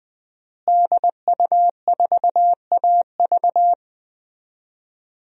Morse code, heard as a keyed tone, about 20 wpm, 700 Hz. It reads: DU4AV
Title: DXCC-100 Call Signs - 5 Characters - Repeated 2x - 20wpm